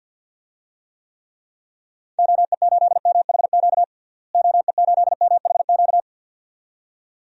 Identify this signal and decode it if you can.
Morse code, heard as a keyed tone, about 53 wpm, 700 Hz. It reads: OE8MHQ OE8MHQ
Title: DXCC-100 Call Signs - 6 Characters - Repeated 2x - 50wpm